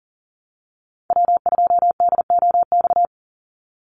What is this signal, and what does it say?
Morse code, heard as a keyed tone, about 40 wpm, 700 Hz. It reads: W2DOX
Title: USA Call Signs 40wpm